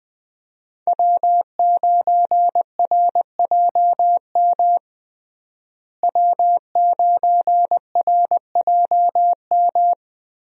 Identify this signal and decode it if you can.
Morse code, heard as a keyed tone, about 20 wpm, 700 Hz. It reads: W9RJM W9RJM
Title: USA Call Signs - 5 Characters - Repeated 2x - 20wpm